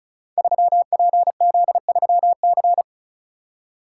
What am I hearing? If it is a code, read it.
Morse code, heard as a keyed tone, about 36 wpm, 700 Hz. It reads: 3PZ3C
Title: Code Groups - 5 Letter:Numbers - 35wpm